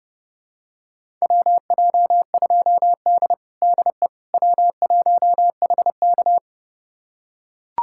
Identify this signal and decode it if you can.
Morse code, heard as a keyed tone, about 31 wpm, 700 Hz. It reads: WJ2D DE W1HK
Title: USA Callsign DE Callsign - 4 Characters - 30wpm